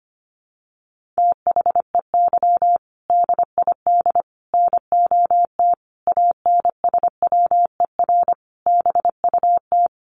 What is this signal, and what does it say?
Morse code, heard as a keyed tone, about 25 wpm, 700 Hz. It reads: THEY DID NOT ANSWER BUT
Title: Sentences from Top 600 Words 25wpm